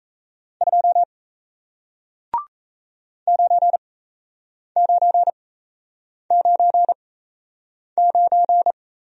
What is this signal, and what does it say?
Morse code, heard as a keyed tone, about 38 wpm, spaced as for 8 wpm, 700 Hz. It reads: 2 9999